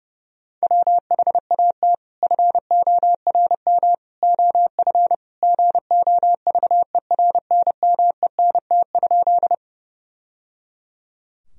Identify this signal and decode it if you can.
Morse code, heard as a keyed tone, about 30 wpm, 700 Hz. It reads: WHAT FORM OF GOVERNMENT?
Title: Sentences from Top 1000 Words - Review Part 3 - 30wpm